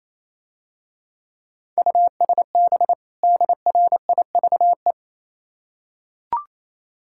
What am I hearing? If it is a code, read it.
Morse code, heard as a keyed tone, about 27 wpm, 700 Hz. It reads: USB DRIVE
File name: Easy Words - Speed-Racing with Hints - 22wpm